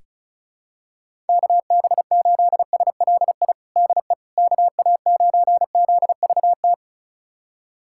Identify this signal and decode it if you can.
Morse code, heard as a keyed tone, about 36 wpm, 700 Hz. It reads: KB8SLI DE KA9ZVT
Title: USA Callsign DE Callsign - 6 Characters - 35wpm